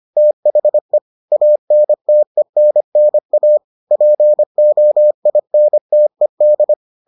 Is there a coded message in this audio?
Yes: Morse code from 0.2 to 6.7 s, heard as a keyed tone, about 25 wpm, 600 Hz. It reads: THE ANTENNA POINTED